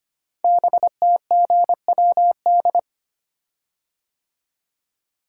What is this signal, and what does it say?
Morse code, heard as a keyed tone, about 25 wpm, 700 Hz. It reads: BTGWD
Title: Code Groups - 5 Letters - 25wpm